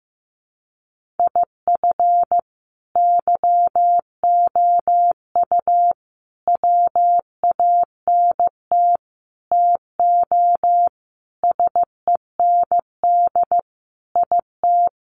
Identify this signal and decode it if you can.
Morse code, heard as a keyed tone, about 15 wpm, 700 Hz. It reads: IF YOU WANT TO SEND IT
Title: Sentences from Top 1500 Words 15wpm